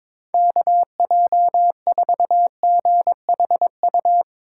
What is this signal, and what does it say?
Morse code, heard as a keyed tone, about 22 wpm, 700 Hz. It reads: KJ4GHU
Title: Speed Racing - USA 2x3 Call Signs 22wpm